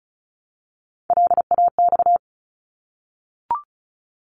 Morse code, heard as a keyed tone, about 36 wpm, 700 Hz. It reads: LAX
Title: ARRL Sections 35wpm